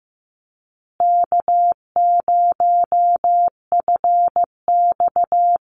Morse code, heard as a keyed tone, about 15 wpm, 700 Hz. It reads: K0FX
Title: Easy Call Signs Repeated 3x - 15wpm